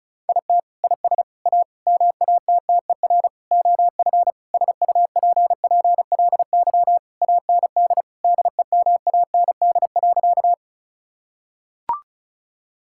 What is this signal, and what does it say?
Morse code, heard as a keyed tone, about 36 wpm, 690 Hz. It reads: IT IS A MATTER OF SUPPLY AND DEMAND.
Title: Sentences from Top 1000 Words - Review Part 5 - 35wpm